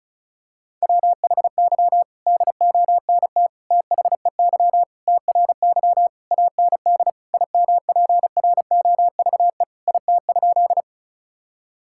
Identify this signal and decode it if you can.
Morse code, heard as a keyed tone, about 36 wpm, 700 Hz. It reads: WHY DONT THEY TRY AND IMPROVE IT?
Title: Sentences from Top 1000 Words - Review Part 5 - 35wpm